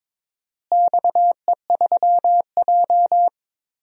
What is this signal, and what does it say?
Morse code, heard as a keyed tone, about 22 wpm, 700 Hz. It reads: XE3J
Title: DXCC-100 Call Signs - 4 Characters - Repeated 2x - 22wpm